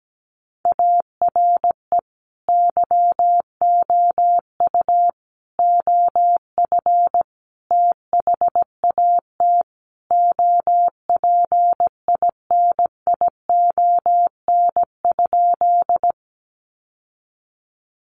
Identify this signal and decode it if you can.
Morse code, heard as a keyed tone, about 17 wpm, 700 Hz. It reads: ARE YOU OF THAT OPINION?
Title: Sentences from Top 1200 Words 17wpm